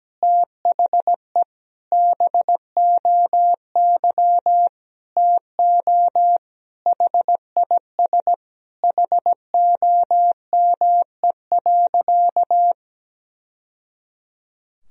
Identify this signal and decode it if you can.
Morse code, heard as a keyed tone, about 17 wpm, 700 Hz. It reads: THE BOY TO HIS HOME.
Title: Sentences from Top 1000 Words - Review Part 1 - 17wpm